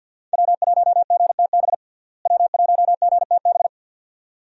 Morse code, heard as a keyed tone, about 45 wpm, 700 Hz. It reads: W1GTB W1GTB
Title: USA Call Signs - 5 Characters - Repeated 2x - 50wpm